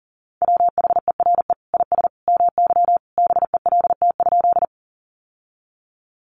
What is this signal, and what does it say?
Morse code, heard as a keyed tone, about 40 wpm, 700 Hz. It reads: WHERE IS MY BELT?
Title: Sentences from Top 1400 Words 40wpm